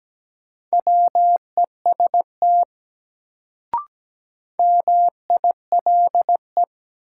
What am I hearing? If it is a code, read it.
Morse code, heard as a keyed tone, about 17 wpm, 700 Hz. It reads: WEST MILE
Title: Sentences from Top 400 Words - New Word Drill - 17wpm